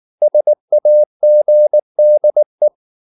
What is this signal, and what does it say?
Morse code, heard as a keyed tone, about 19 wpm, 600 Hz. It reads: SAGDE
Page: Ferie 19 wpm